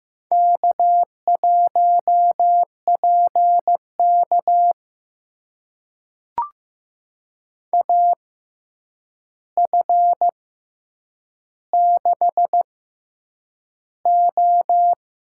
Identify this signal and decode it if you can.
Morse code, heard as a keyed tone, about 15 wpm, spaced as for 8 wpm, 700 Hz. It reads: K1PK AF6O
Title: USA Call Signs - 6x Character Spacing - 15wpm